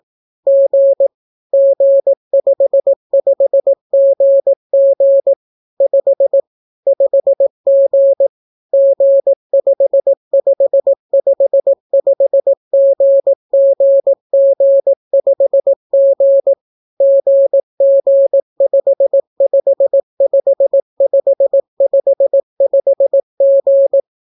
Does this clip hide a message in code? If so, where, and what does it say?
Yes: Morse code from 0.5 to 24.0 s, heard as a keyed tone, about 18 wpm, 550 Hz. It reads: G G55GG 5 5G G5555GGG5G GG555555G